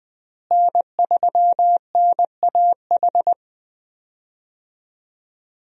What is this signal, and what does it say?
Morse code, heard as a keyed tone, about 20 wpm, 700 Hz. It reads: N3NAH
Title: USA Call Signs - 5 Characters - Repeated 2x - 20wpm